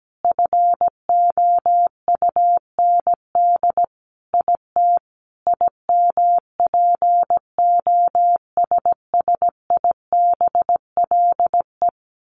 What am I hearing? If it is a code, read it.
Morse code, heard as a keyed tone, about 17 wpm, 700 Hz. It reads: FOUND IT IMPOSSIBLE
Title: Sentences from Top 1100 Words 17wpm